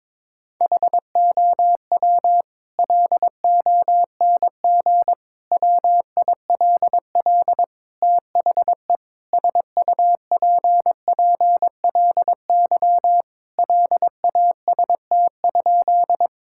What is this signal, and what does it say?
Morse code, heard as a keyed tone, about 22 wpm, 700 Hz. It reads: HOW LONG WILL THE SUPPLY LAST?
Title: Sentences from Top 1000 Words - Review Part 5 - 22wpm